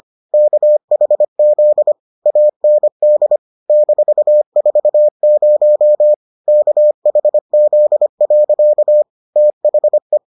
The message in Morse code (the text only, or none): KHZ AND -40 KHZ. THE